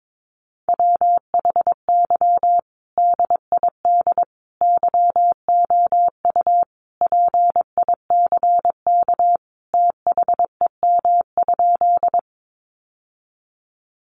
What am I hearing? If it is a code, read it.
Morse code, heard as a keyed tone, about 22 wpm, 700 Hz. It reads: WHY DID YOU PICK THEM?
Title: Sentences from Top 1000 Words 22wpm